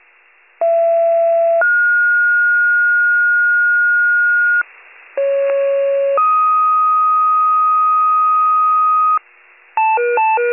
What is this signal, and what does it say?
Two-tone page, heard as a one-second tone, then a three-second tone, about 670 then 1430 Hz, about 550 then 1250 Hz, with noise. It is heard from 0.6 to 9.2 s.